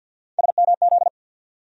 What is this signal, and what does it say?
Morse code, heard as a keyed tone, about 51 wpm, 700 Hz. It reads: SMZ
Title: USA Call Sign Postfixes - 3 Letters 50wpm